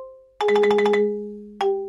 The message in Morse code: none